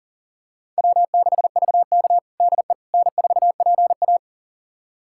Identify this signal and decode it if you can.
Morse code, heard as a keyed tone, about 40 wpm, 700 Hz. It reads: W6VK DE N4PA
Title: USA Callsign DE Callsign - 4 Characters - 40wpm